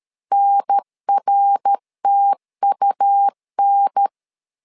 Morse code, heard as a keyed tone, about 13 wpm, 800 Hz. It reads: NRTUN
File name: LEZIONI / 3 MWRO / 3 MWRO 12wpm 1730